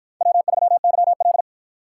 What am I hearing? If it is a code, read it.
Morse code, heard as a keyed tone, about 48 wpm, 700 Hz. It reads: W3YL